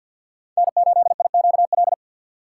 Morse code, heard as a keyed tone, about 52 wpm, 700 Hz. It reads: N8IQL
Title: USA Call Signs - 2x Character Spacing - 50wpm